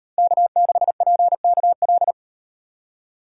Morse code, heard as a keyed tone, about 35 wpm, 700 Hz. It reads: K6PKL